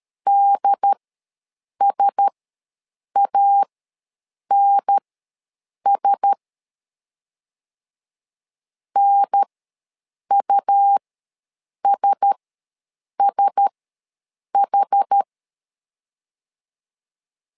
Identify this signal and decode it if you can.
Morse code, heard as a keyed tone, about 13 wpm, spaced as for 7 wpm, 800 Hz. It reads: DSANS NUSSH